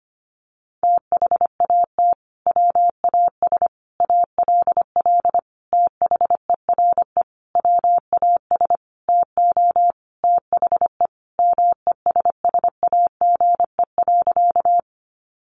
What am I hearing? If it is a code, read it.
Morse code, heard as a keyed tone, about 25 wpm, 700 Hz. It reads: THAT WAS ALL THERE WAS TO THE MESSAGE.
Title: Sentences from Top 1100 Words 25wpm